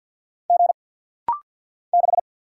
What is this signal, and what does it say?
Morse code, heard as a keyed tone, about 50 wpm, 700 Hz. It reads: G 6
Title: Single Letter-Number - Rapid-Fire 50wpm